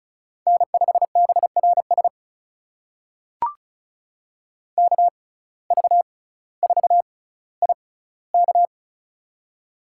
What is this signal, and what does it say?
Morse code, heard as a keyed tone, about 36 wpm, spaced as for 18 wpm, 700 Hz. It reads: N5BRS KV4IK
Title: USA Call Signs - 5 Characters - 6x Character Spacing - 35wpm